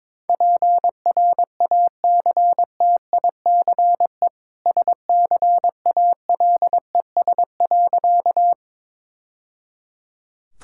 Morse code, heard as a keyed tone, about 21 wpm, 690 Hz. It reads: PRACTICE SCALES.
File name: practice / 22wpm / Easy Sentences with Hints - 22wpm